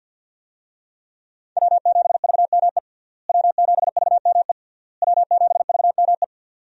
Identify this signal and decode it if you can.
Morse code, heard as a keyed tone, about 51 wpm, 700 Hz. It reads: W7VME W7VME W7VME
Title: USA Call Signs - 5 Characters - Repeated 3x - 50wpm